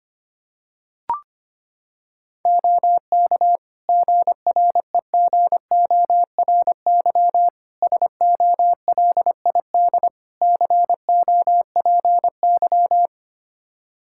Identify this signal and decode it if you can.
Morse code, heard as a keyed tone, about 25 wpm, 700 Hz. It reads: OK GREGORY SOLID COPY
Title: Practice QSO - Exchange 3 - More Info - 25wpm